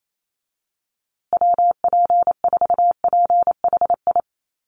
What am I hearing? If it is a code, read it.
Morse code, heard as a keyed tone, about 28 wpm, 700 Hz. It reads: WP4PHI
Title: USA Call Signs Repeated 3x 28wpm